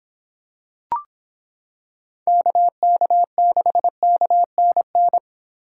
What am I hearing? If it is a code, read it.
Morse code, heard as a keyed tone, about 26 wpm, 700 Hz. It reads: KK6KNN